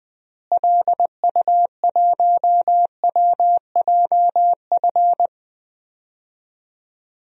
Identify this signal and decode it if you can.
Morse code, heard as a keyed tone, about 20 wpm, 700 Hz. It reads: LU1WJF
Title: DXCC-100 Call Signs - 6 Characters - Repeated 2x - 20wpm